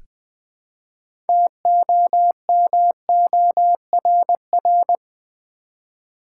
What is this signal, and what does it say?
Morse code, heard as a keyed tone, about 20 wpm, 700 Hz. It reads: TOMORR
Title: Build A Word - 3 Letters At A Time - 8-Letter Words - 20wpm